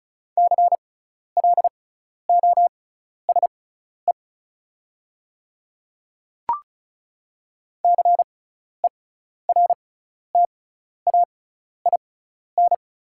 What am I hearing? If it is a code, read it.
Morse code, heard as a keyed tone, about 36 wpm, spaced as for 12 wpm, 690 Hz. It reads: CLOSE CERTAIN
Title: 6x Character Spacing - Top 500 Words - 35wpm